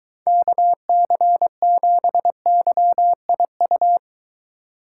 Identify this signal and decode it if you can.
Morse code, heard as a keyed tone, about 23 wpm, 700 Hz. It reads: KC7YIU